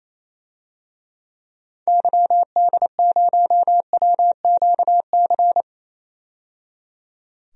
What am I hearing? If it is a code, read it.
Morse code, heard as a keyed tone, about 28 wpm, 700 Hz. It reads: YD0WQC
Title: DXCC-100 Call Signs - 6 Characters - 28wpm